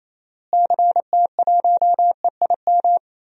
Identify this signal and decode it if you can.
Morse code, heard as a keyed tone, about 27 wpm, 700 Hz. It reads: CT1EIM
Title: DXCC-100 Call Signs - 6 Characters - Repeated 3x - 28wpm